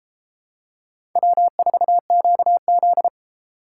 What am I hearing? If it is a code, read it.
Morse code, heard as a keyed tone, about 32 wpm, 690 Hz. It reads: W4QZ